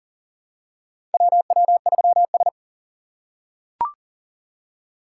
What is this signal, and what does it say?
Morse code, heard as a keyed tone, about 40 wpm, 700 Hz. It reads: WW3S
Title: USA Call Signs - 4 Characters - 3x Character Spacing - 40wpm